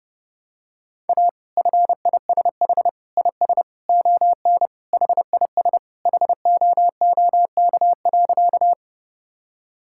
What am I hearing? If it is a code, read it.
Morse code, heard as a keyed tone, about 30 wpm, 700 Hz. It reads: A FISH IS ON HIS HOOK.
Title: Sentences from Top 1600 Words 30wpm